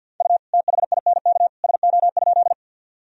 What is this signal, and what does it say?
Morse code, heard as a keyed tone, about 52 wpm, 700 Hz. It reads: U THINK SO?
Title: Sentences from Top 200 Words and Abbreviations 50wpm